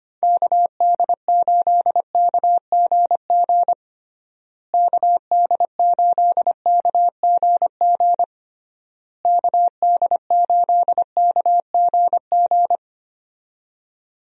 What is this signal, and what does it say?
Morse code, heard as a keyed tone, about 25 wpm, 700 Hz. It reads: KD8KGG KD8KGG KD8KGG
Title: USA Call Signs - 6 Characters - Repeated 3x - 25wpm